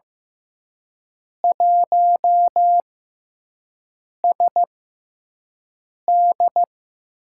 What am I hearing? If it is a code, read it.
Morse code, heard as a keyed tone, about 15 wpm, spaced as for 5 wpm, 700 Hz. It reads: 1SD
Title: USA Call Signs - 6x Character Spacing - 15wpm